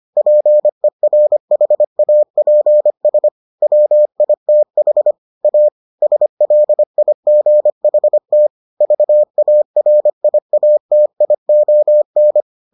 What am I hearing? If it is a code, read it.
Morse code, heard as a keyed tone, about 25 wpm, 600 Hz. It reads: PERHAPS WITH A SLIGHT VARIATION